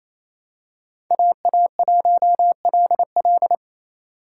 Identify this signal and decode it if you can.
Morse code, heard as a keyed tone, about 28 wpm, 700 Hz. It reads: AA1LL
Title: USA Call Signs - 5 Characters - 6x Character Spacing - 28wpm